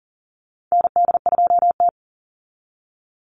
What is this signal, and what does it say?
Morse code, heard as a keyed tone, about 40 wpm, 700 Hz. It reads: ND2T
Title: Call Signs Repeated 2x - 40wpm